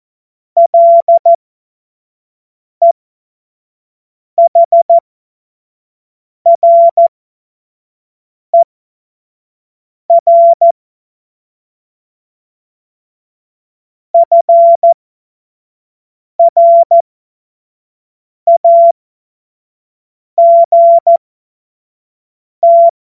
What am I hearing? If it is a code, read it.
Morse code, heard as a keyed tone, about 14 wpm, spaced as for 5 wpm, 680 Hz. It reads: LEHRER FRAGT
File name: Morsekurs / Witze / Witz 28. September 2021, 25 F70